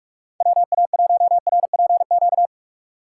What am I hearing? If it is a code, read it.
Morse code, heard as a keyed tone, about 44 wpm, 700 Hz. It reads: WA1RPQ